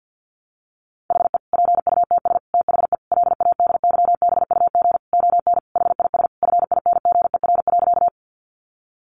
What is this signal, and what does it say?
Morse code, heard as a keyed tone, about 50 wpm, 700 Hz. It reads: HE PUTS THE LADYBUG ON HIS FINGER.
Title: Additional Easy Phrases 50wpm